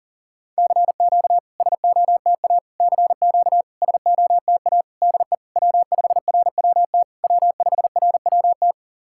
Morse code, heard as a keyed tone, about 40 wpm, 700 Hz. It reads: CQ SOTA CQ SOTA DE W5RWT W5RWT
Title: SOTA QSO Practice 40wpm